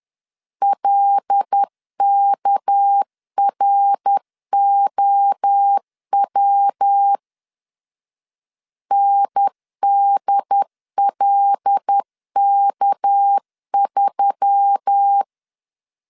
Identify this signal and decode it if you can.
Morse code, heard as a keyed tone, about 10 wpm, 800 Hz. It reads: LKROW NDLK3
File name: CW / DET / 1 6 1500C S B 10wpm 4137